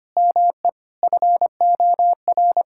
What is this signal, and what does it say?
Morse code, heard as a keyed tone, about 25 wpm, 700 Hz. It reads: ME FOR
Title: Sentences from Top 1000 Words - Review Part 6 - 25wpm